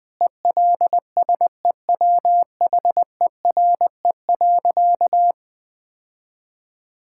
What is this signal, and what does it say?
Morse code, heard as a keyed tone, about 20 wpm, 700 Hz. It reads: ELSEWHERE.
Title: Sentences from Top 1600 Words 20wpm